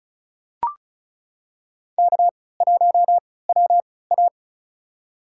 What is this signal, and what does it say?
Morse code, heard as a keyed tone, about 36 wpm, spaced as for 20 wpm, 700 Hz. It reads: K1WA
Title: USA Call Signs - 3x Character Spacing - 35wpm